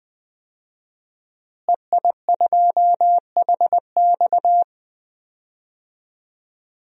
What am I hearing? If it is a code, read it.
Morse code, heard as a keyed tone, about 20 wpm, 700 Hz. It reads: EI2HX
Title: DXCC-100 Call Signs - 5 Characters - Repeated 2x - 20wpm